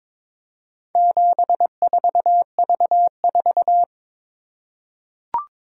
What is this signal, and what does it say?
Morse code, heard as a keyed tone, about 22 wpm, 700 Hz. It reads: 74V4
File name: practice / 22wpm / Four Letter-Number Combinations - 22wpm